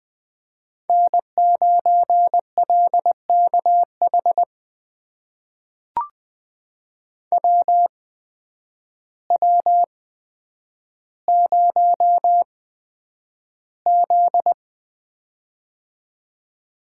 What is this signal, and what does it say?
Morse code, heard as a keyed tone, about 20 wpm, spaced as for 9 wpm, 700 Hz. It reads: N9LKH WW0Z
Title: USA Call Signs - 8x Character Spacing - 20wpm